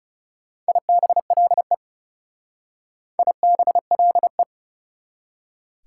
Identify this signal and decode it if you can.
Morse code, heard as a keyed tone, about 37 wpm, 690 Hz. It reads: IBLE IBLE
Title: Speed-Racing Prefixes and Suffixes 30wpm